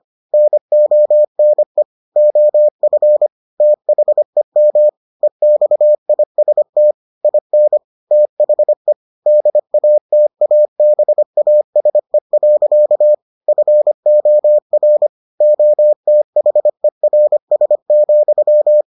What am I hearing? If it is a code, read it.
Morse code, heard as a keyed tone, about 25 wpm, 600 Hz. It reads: NONE OF THEM EXIST IN THE DATABASE. FOR OTHERS,